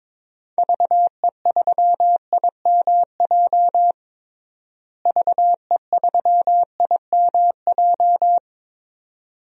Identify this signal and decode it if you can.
Morse code, heard as a keyed tone, about 22 wpm, 700 Hz. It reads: VE3IMJ VE3IMJ
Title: DXCC-100 Call Signs - 6 Characters - Repeated 2x - 22wpm